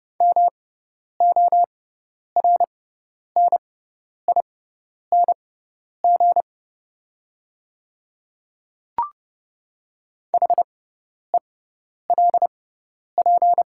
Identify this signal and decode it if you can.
Morse code, heard as a keyed tone, about 30 wpm, spaced as for 10 wpm, 700 Hz. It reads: MORNING HELP
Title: 6x Character Spacing - Top 500 Words - 30wpm